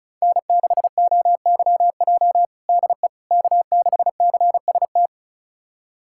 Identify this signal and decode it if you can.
Morse code, heard as a keyed tone, about 36 wpm, 700 Hz. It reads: N6OYJ DE K6CST
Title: USA Callsign DE Callsign - 5 Characters - 35wpm